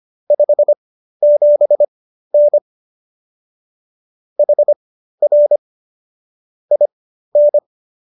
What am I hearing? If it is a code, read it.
Morse code, heard as a keyed tone, about 25 wpm, spaced as for 13 wpm, 590 Hz. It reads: 57N HR IN